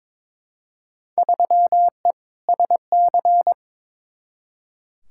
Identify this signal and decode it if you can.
Morse code, heard as a keyed tone, about 22 wpm, 700 Hz. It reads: 3E SC
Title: Field Day Exchanges Encore - 22wpm